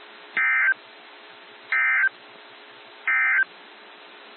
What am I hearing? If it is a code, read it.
Emergency Alert System (SAME) tones, heard as harsh buzzing bursts. It is heard from 0.2 to 3.5 s.
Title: NOAA Weather Radio - WXJ73